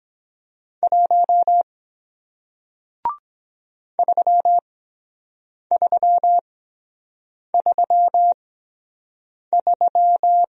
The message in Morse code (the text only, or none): 1 3333